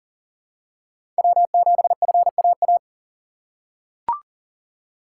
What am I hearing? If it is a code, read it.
Morse code, heard as a keyed tone, about 38 wpm, 700 Hz. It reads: W7FAA
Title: USA Call Signs - 5 Characters - 4x Character Spacing - 40wpm